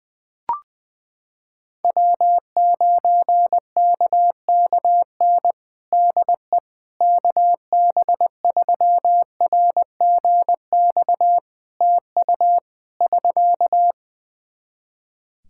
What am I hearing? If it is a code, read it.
Morse code, heard as a keyed tone, about 20 wpm, 700 Hz. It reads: W9KKN DE KB3RGX TU <SK>
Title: Practice QSO - Exchange 4 - Ending - 20wpm